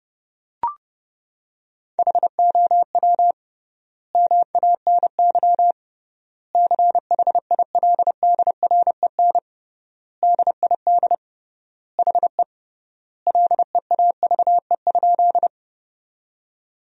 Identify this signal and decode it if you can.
Morse code, heard as a keyed tone, about 30 wpm, 700 Hz. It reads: HOW MANY CHILDREN DID HE LEAVE?
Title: 3x Word-Spacing - Sentences from Top 400 Words - 30wpm